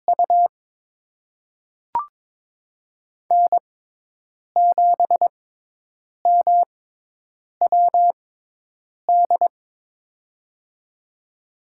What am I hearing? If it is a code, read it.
Morse code, heard as a keyed tone, about 22 wpm, spaced as for 8 wpm, 700 Hz. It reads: U N7MWD
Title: USA Call Signs - 5 Characters - 6x Character Spacing - 22wpm